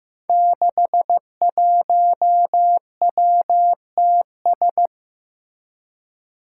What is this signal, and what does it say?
Morse code, heard as a keyed tone, about 15 wpm, 700 Hz. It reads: 61WTS
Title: Five Challenging Letter-Number Combinations 15wpm